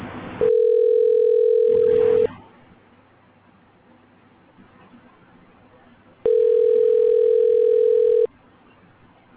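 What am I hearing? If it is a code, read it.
Ringback tone (North America), 440 + 480 Hz, with speech.
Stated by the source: Outbound telephone call.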